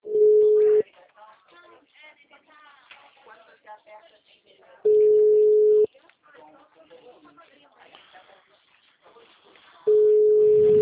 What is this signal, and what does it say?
Ringback tone (Europe, 425 Hz), with speech.